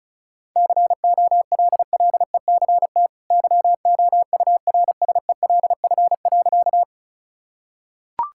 Morse code, heard as a keyed tone, about 36 wpm, 700 Hz. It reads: COLLECT YOURSELF.
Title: Sentences from Top 1600 Words 35wpm